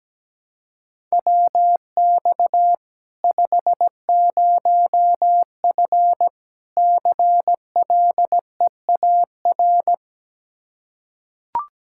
Morse code, heard as a keyed tone, about 17 wpm, 700 Hz. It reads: WX 50F CLEAR
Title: K1USN SST QSO Practice 17wpm